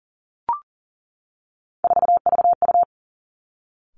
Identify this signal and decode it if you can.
Morse code, heard as a keyed tone, about 39 wpm, 700 Hz. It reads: 4VU
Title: Three Challenging Letter-Number Combinations 40wpm